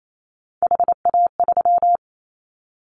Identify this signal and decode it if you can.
Morse code, heard as a keyed tone, about 28 wpm, 700 Hz. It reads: HA3
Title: Callsign Prefixes for Top 50 DXCC - 28wpm